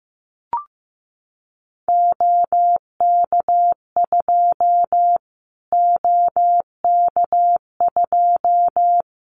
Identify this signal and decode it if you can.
Morse code, heard as a keyed tone, about 15 wpm, 700 Hz. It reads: OK2 OK2
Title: Callsign Prefixes for Top 25 DXCC - Repeated 3x - 15wpm